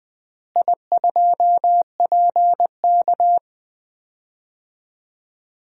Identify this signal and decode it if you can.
Morse code, heard as a keyed tone, about 20 wpm, 700 Hz. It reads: I2PK
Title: DXCC-100 Call Signs - 4 Characters - Repeated 3x - 20wpm